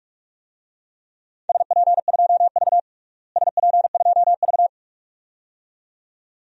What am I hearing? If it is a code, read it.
Morse code, heard as a keyed tone, about 46 wpm, 700 Hz. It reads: SP2V SP2V
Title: DXCC-100 Call Signs - 4 Characters - Repeated 2x - 45wpm